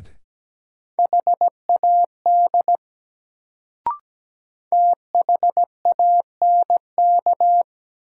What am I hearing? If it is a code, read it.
Morse code, heard as a keyed tone, about 17 wpm, 700 Hz. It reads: HAD THANK
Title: Sets of 2 Words - New Word Drill - 17wpm